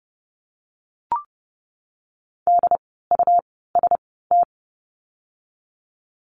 Morse code, heard as a keyed tone, about 30 wpm, spaced as for 17 wpm, 700 Hz. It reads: DUST
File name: Lesson 17f - Letter u - Words 30wpm